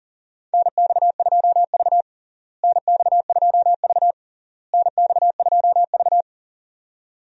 Morse code, heard as a keyed tone, about 40 wpm, 700 Hz. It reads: NX2V NX2V NX2V
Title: USA Call Signs - 4 Characters - Repeated 3x - 40wpm